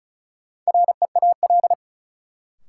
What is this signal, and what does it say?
Morse code, heard as a keyed tone, about 35 wpm, 700 Hz. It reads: REAL
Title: Sentences from Top 300 Words - New Word Drill - 35wpm